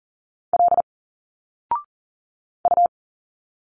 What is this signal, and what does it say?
Morse code, heard as a keyed tone, about 39 wpm, 700 Hz. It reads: L U
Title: Single Letters 40wpm